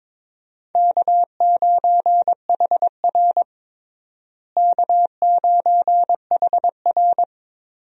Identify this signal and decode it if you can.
Morse code, heard as a keyed tone, about 22 wpm, 700 Hz. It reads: K9HR K9HR
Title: USA Call Signs - 4 Characters - Repeated 3x - 22wpm